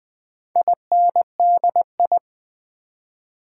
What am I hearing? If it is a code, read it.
Morse code, heard as a keyed tone, about 20 wpm, 700 Hz. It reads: INDI
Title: Build A Word - 4 Letters At A Time - 10-Letter Words - 20wpm